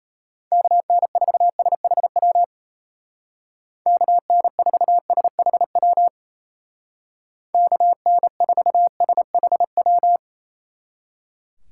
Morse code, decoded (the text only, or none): KN4SHW KN4SHW KN4SHW